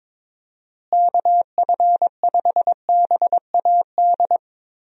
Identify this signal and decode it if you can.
Morse code, heard as a keyed tone, about 21 wpm, 700 Hz. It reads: KF5BAD
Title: USA Call Signs - 6 Characters - Repeated 2x - 22wpm